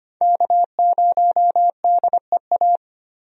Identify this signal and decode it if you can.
Morse code, heard as a keyed tone, about 25 wpm, 700 Hz. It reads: K0DEA
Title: USA Call Signs - 5 Characters - 25wpm